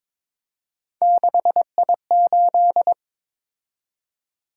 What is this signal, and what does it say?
Morse code, heard as a keyed tone, about 22 wpm, 700 Hz. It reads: 6I8
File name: practice / 22wpm / Three Letter-Number Combinations - 22wpm